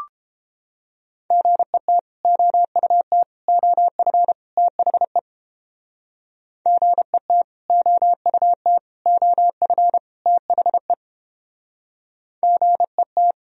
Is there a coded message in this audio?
Yes: Morse code from 1.3 to 13.3 s, heard as a keyed tone, about 31 wpm, 700 Hz. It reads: GET OUT OF THE GET OUT OF THE GET